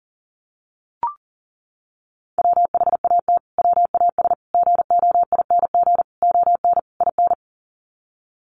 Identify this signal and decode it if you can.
Morse code, heard as a keyed tone, about 37 wpm, 700 Hz. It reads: WHAT WAS GOING ON IN
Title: Sets of 5 Words Encore 40wpm